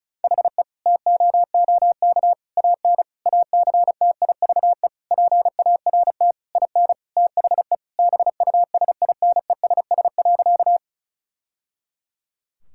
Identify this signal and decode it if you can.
Morse code, heard as a keyed tone, about 36 wpm, 700 Hz. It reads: HE TOOK AN ACTIVE PART IN THE BUSINESS.
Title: Sentences from Top 1000 Words - Review Part 6 - 35wpm